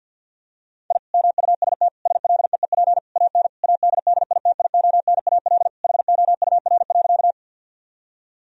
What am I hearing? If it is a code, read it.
Morse code, heard as a keyed tone, about 50 wpm, 700 Hz. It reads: I MUST SLEEP AN ADDITIONAL HOUR.
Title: Sentences from Top 1000 Words - Review Part 4 - 50wpm